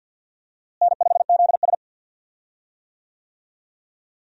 Morse code, heard as a keyed tone, about 53 wpm, 700 Hz. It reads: N5ZS
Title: USA Call Signs - 4 Characters - Repeated 2x - 50wpm